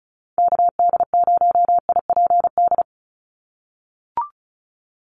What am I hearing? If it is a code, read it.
Morse code, heard as a keyed tone, about 34 wpm, 700 Hz. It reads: KD0IPD
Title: USA Call Signs 35wpm